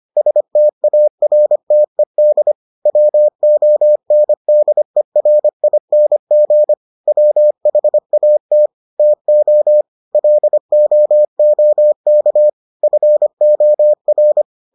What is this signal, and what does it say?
Morse code, heard as a keyed tone, about 25 wpm, 600 Hz. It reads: STARTED WONDERING WHAT TO LOOK FOR